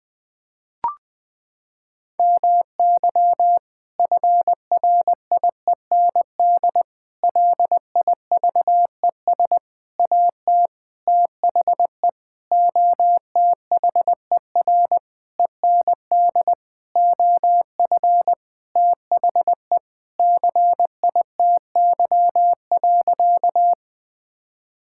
Morse code, heard as a keyed tone, about 21 wpm, 700 Hz. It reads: MY FRIEND LIVES AT THE OTHER END OF THE CITY.
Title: Sentences with 5-Word Phrases - 20wpm